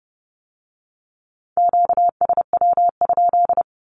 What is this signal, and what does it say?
Morse code, heard as a keyed tone, about 30 wpm, 700 Hz. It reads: QSW?
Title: Q-Codes with Spoken Meaning - 30wpm